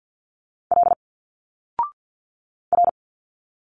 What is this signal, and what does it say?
Morse code, heard as a keyed tone, about 50 wpm, 700 Hz. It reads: L R
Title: Single Letters 50wpm